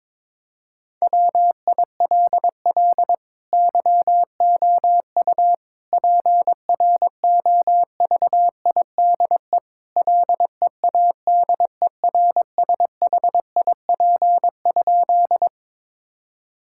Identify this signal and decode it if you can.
Morse code, heard as a keyed tone, about 22 wpm, 700 Hz. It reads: WILL YOU PROVIDE LEADERSHIP?
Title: Sentences from Top 1600 Words 22wpm